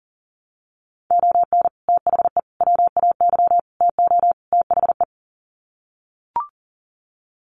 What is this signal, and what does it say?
Morse code, heard as a keyed tone, about 40 wpm, 700 Hz. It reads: ON THE WAY TO THE
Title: Sets of 5 Words 40wpm